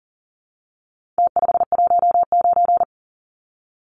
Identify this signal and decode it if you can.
Morse code, heard as a keyed tone, about 40 wpm, 700 Hz. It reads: T519
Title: Numbers - Contest Style - 4-Digit 40wpm